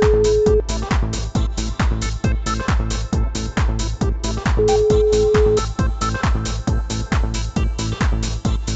Ringback tone (Europe, 425 Hz), with noise.